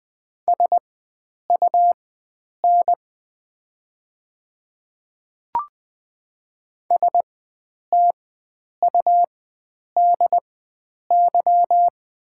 Morse code, heard as a keyed tone, about 20 wpm, spaced as for 9 wpm, 700 Hz. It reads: SUN STUDY
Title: 4x Character Spacing - Top 300 Words - 20wpm